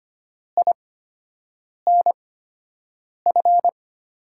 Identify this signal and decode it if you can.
Morse code, heard as a keyed tone, about 25 wpm, spaced as for 7 wpm, 700 Hz. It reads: INF
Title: USA Call Signs - 5 Characters - 8x Character Spacing - 25wpm